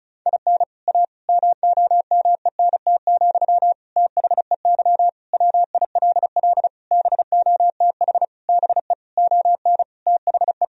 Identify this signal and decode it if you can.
Morse code, heard as a keyed tone, about 34 wpm, 700 Hz. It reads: IN A MOMENT, THEY WILL BOTH BE ON THE
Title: Sentences from Top 1000 Words - Review Part 2 - 35wpm